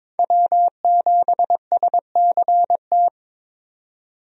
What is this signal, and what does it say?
Morse code, heard as a keyed tone, about 22 wpm, 700 Hz. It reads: W7SCT
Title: USA Call Signs - 5 Characters - 22wpm